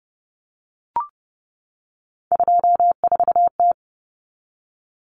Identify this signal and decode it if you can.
Morse code, heard as a keyed tone, about 29 wpm, 700 Hz. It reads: 24T
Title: Common Cut Numbers - 3-Digit 30wpm